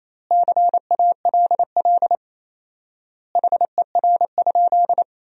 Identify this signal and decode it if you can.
Morse code, heard as a keyed tone, about 28 wpm, 700 Hz. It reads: CALL HER?
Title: 4x Word-Spacing - Sentences from Top 100 Words - 28wpm